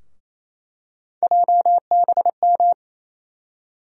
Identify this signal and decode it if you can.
Morse code, heard as a keyed tone, about 28 wpm, 700 Hz. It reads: JBM
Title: USA Call Sign Postfixes - 3 Letters 28wpm